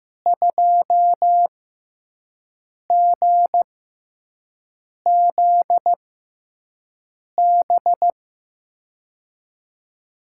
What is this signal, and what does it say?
Morse code, heard as a keyed tone, about 15 wpm, spaced as for 5 wpm, 700 Hz. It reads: 2GZB
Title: USA Call Signs - 6x Character Spacing - 15wpm